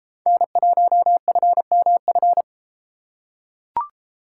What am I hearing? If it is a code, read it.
Morse code, heard as a keyed tone, about 33 wpm, 700 Hz. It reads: N1FMF